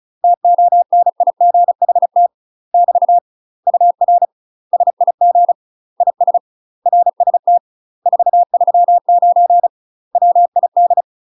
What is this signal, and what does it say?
Morse code, heard as a keyed tone, about 35 wpm, 700 Hz. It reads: TONIGHT <BT> UR SIG IS RST 439 WID
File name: UK MASTER QSOs TRAINING / 35WPM / UK MASTER QSOs 0007